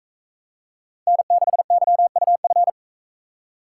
Morse code, heard as a keyed tone, about 41 wpm, 700 Hz. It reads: N6YUF